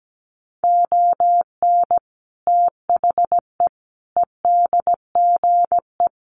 Morse code, heard as a keyed tone, about 17 wpm, 700 Hz. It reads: ON THE EDGE
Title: Sets of 5 Words Encore2 17wpm